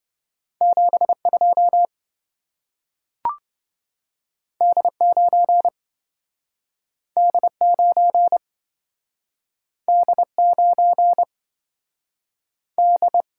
Morse code, heard as a keyed tone, about 27 wpm, 700 Hz. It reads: 72 D9 D9 D9 D